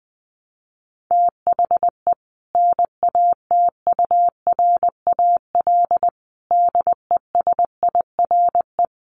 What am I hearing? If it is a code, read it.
Morse code, heard as a keyed tone, about 20 wpm, 700 Hz. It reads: THE NATURAL DESIRE
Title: Sentences from Top 1100 Words 20wpm